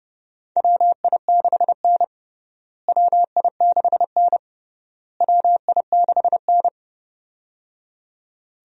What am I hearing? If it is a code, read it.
Morse code, heard as a keyed tone, about 30 wpm, 700 Hz. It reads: WI6N WI6N WI6N
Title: USA Call Signs - 4 Characters - Repeated 3x - 30wpm